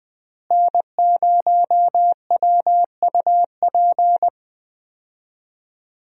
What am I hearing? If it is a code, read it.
Morse code, heard as a keyed tone, about 20 wpm, 700 Hz. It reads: N0WUP
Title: USA Call Signs - 5 Characters - Repeated 2x - 20wpm